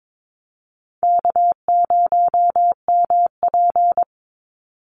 Morse code, heard as a keyed tone, about 22 wpm, 700 Hz. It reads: K0MP
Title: NA SOTA Chaser Callsigns 22wpm